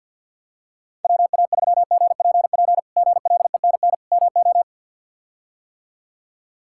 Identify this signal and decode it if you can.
Morse code, heard as a keyed tone, about 52 wpm, 700 Hz. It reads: WA3GPP GLENN MO
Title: K1USN SST QSO Practice - 50wpm